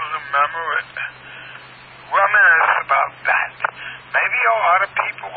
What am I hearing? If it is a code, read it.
Emergency Alert System (SAME) tones, heard as harsh buzzing bursts, with speech. It is heard from 0.8 to 4.2 s.